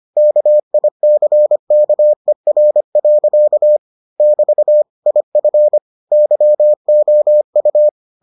Morse code, heard as a keyed tone, about 25 wpm, 600 Hz. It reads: KICKER. <BT> IF YOU